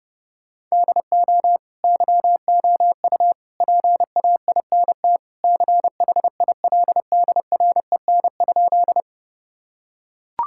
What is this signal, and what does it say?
Morse code, heard as a keyed tone, about 30 wpm, 700 Hz. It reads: DO YOU PAINT CHILDREN?
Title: Sentences from Top 1000 Words - Review Part 6 - 30wpm